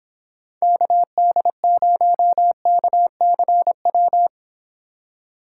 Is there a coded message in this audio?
Yes: Morse code from 0.6 to 4.3 s, heard as a keyed tone, about 29 wpm, 700 Hz. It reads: KD0KCW